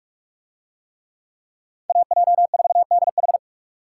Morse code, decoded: AJ4DH

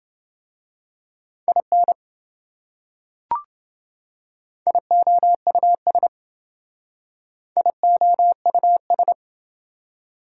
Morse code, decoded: IN IOUS IOUS

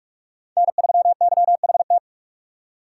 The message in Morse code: N3YHT

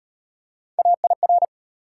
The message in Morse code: AIR

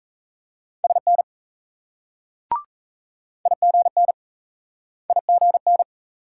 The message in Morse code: SN IGN IGN

